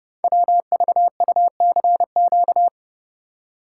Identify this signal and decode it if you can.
Morse code, heard as a keyed tone, about 29 wpm, 700 Hz. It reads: WVUCQ
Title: Code Groups - 5 Letters - 30wpm